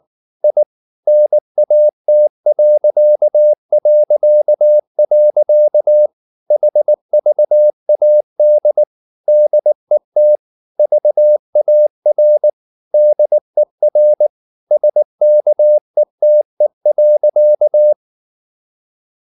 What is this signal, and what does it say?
Morse code, heard as a keyed tone, about 19 wpm, 600 Hz. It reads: I NAT... HVAD DET VAR DER SKETE.